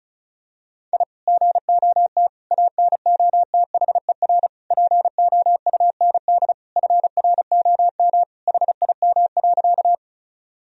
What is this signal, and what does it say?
Morse code, heard as a keyed tone, about 36 wpm, 700 Hz. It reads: I GOT ANOTHER POUND FROM HIM.
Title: Sentences from Top 1000 Words - Review Part 2 - 35wpm